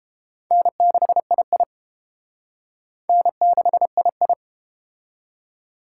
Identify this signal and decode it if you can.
Morse code, heard as a keyed tone, about 30 wpm, 700 Hz. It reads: N6II N6II